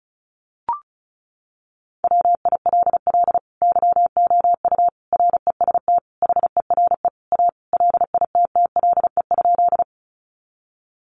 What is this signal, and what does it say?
Morse code, heard as a keyed tone, about 34 wpm, 700 Hz. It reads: WILL YOU REST HERE A LITTLE?
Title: Sentences from Top 500 Words 35wpm